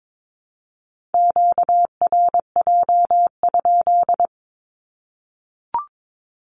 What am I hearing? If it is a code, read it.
Morse code, heard as a keyed tone, about 22 wpm, 700 Hz. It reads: QRJ?
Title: Q-Codes Repeated 3x 22wpm